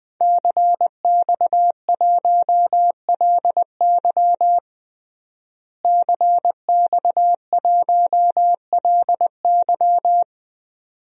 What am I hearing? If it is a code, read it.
Morse code, heard as a keyed tone, about 20 wpm, 700 Hz. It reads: CX1LY CX1LY
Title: DXCC-100 Call Signs - 5 Characters - Repeated 2x - 20wpm